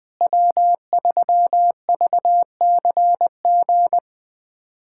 Morse code, heard as a keyed tone, about 20 wpm, 700 Hz. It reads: W3VCG
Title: USA Callsign DE Callsign 20wpm